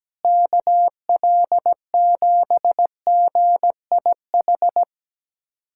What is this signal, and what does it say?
Morse code, heard as a keyed tone, about 17 wpm, 700 Hz. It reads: KL7GIH
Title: USA Callsign DE Callsign - 6 Characters - 17wpm